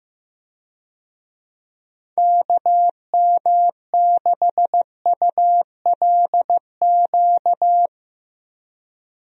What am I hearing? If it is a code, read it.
Morse code, heard as a keyed tone, about 15 wpm, 700 Hz. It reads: KM6ULQ
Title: USA Call Signs - 6 Characters - Repeated 3x - 15wpm